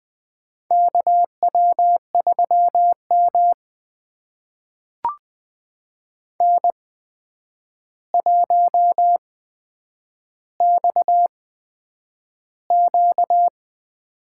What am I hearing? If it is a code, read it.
Morse code, heard as a keyed tone, about 20 wpm, spaced as for 9 wpm, 700 Hz. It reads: KW3M N1XQ
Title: USA Call Signs - 4 Characters - 8x Character Spacing - 20wpm